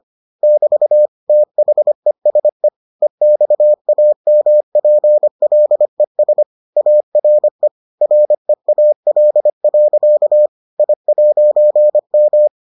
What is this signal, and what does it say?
Morse code, heard as a keyed tone, about 25 wpm, 600 Hz. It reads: <BT> THESE EXAMPLES ARE REAL. I'M